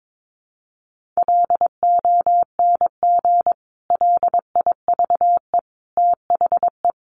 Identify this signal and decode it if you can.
Morse code, heard as a keyed tone, about 22 wpm, 700 Hz. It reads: LONG LIVE THE
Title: Sentences from Top 1500 Words 22wpm